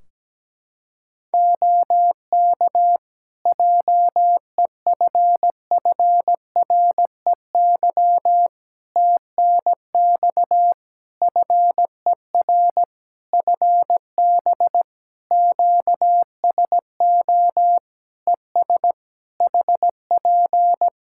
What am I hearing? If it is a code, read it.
Morse code, heard as a keyed tone, about 17 wpm, 700 Hz. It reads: OK JEFFREY TNX FER FB QSO ES HP
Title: Practice QSO - Exchange 4 - Ending - 17wpm